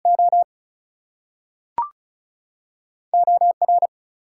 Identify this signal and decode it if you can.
Morse code, heard as a keyed tone, about 36 wpm, 700 Hz. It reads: O OR
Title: Two Letter Words - 35wpm